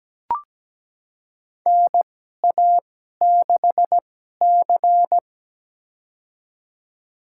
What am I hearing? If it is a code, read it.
Morse code, heard as a keyed tone, about 17 wpm, spaced as for 12 wpm, 700 Hz. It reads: NA6C
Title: USA Call Signs - 4 Characters - 2x Character Spacing - 17wpm